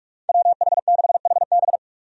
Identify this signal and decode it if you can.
Morse code, heard as a keyed tone, about 45 wpm, 700 Hz. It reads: WH6HB
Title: USA Call Signs - 5 Characters - 45wpm